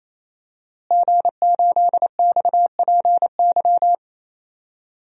Morse code, heard as a keyed tone, about 28 wpm, 700 Hz. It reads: G8XPY
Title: DXCC-100 Call Signs - 5 Characters - Repeated 3x - 28wpm